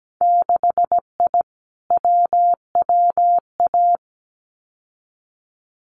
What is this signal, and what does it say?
Morse code, heard as a keyed tone, about 17 wpm, 700 Hz. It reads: 6I WWA
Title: Winter Field Day Exchanges Encore 17wpm